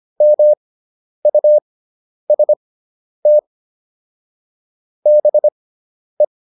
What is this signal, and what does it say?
Morse code, heard as a keyed tone, about 25 wpm, spaced as for 10 wpm, 600 Hz. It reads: MUST BE